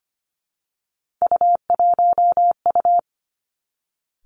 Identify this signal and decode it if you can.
Morse code, heard as a keyed tone, about 25 wpm, 700 Hz. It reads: U1U
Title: Three Challenging Letter-Number Combinations 25wpm